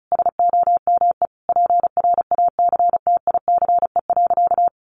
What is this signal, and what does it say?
Morse code, heard as a keyed tone, about 34 wpm, 700 Hz. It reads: SOME PRACTICE.
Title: Sentences from Top 900 Words 35wpm